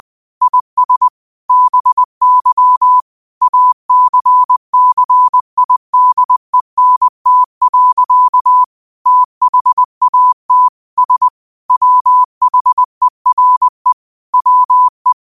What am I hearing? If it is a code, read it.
Morse code, heard as a keyed tone, about 20 wpm, 1000 Hz. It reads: IS BY ACCIDENT. THAT S WHERE WE